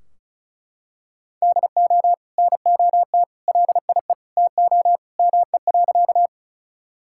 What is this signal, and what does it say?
Morse code, heard as a keyed tone, about 34 wpm, 700 Hz. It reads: DO NOT LIE TO ME.
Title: Sentences from Top 1600 Words 35wpm